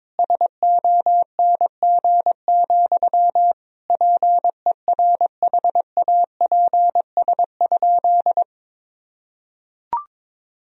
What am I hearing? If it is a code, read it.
Morse code, heard as a keyed tone, about 21 wpm, 700 Hz. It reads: SONG, PERHAPS?
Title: Sentences from Top 1000 Words - Review Part 2 - 22wpm